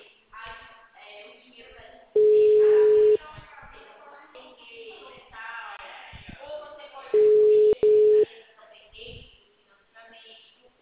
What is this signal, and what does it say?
Ringback tone (Europe, 425 Hz), with speech.
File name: out- -2044- - - .55216